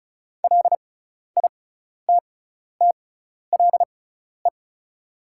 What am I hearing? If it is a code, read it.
Morse code, heard as a keyed tone, about 36 wpm, spaced as for 12 wpm, 700 Hz. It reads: LITTLE
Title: 6x Character Spacing - Top 500 Words - 35wpm